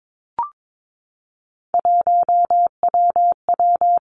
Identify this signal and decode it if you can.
Morse code, heard as a keyed tone, about 22 wpm, 700 Hz. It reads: 1WW
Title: Three Challenging Letter-Number Combinations 22wpm